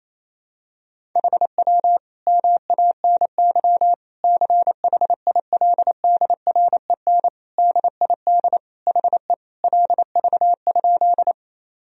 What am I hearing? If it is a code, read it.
Morse code, heard as a keyed tone, about 27 wpm, 700 Hz. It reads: HW MANY CHILDREN DID HE LV?
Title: Sentences from Top 400 Words and Abbreviations 28wpm